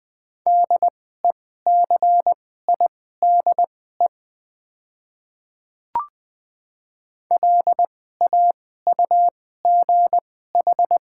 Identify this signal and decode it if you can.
Morse code, heard as a keyed tone, about 20 wpm, spaced as for 14 wpm, 690 Hz. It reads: DECIDE LAUGH
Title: 2x Character Spacing - Top 500 Words - 20wpm